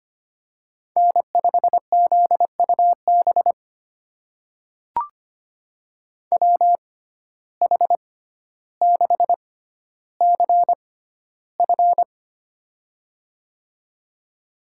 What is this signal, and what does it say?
Morse code, heard as a keyed tone, about 25 wpm, spaced as for 13 wpm, 700 Hz. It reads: N5ZUB WH6CF
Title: USA Call Signs - 5 Characters - 6x Character Spacing - 25wpm